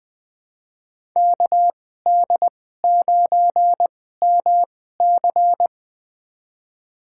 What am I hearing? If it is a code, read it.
Morse code, heard as a keyed tone, about 20 wpm, spaced as for 14 wpm, 700 Hz. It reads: KD9MC
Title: USA Call Signs - 5 Characters - 2x Character Spacing - 20wpm